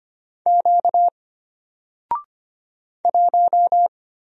Morse code, heard as a keyed tone, about 25 wpm, 700 Hz. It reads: Q 1
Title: Single Letter-Number Alt 25wpm